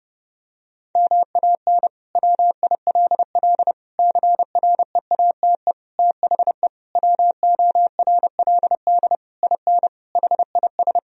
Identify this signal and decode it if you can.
Morse code, heard as a keyed tone, about 30 wpm, 700 Hz. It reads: MAN WILL CREATE THE WORLD IN HIS
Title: Sentences from Top 1000 Words - Review Part 4 - 30wpm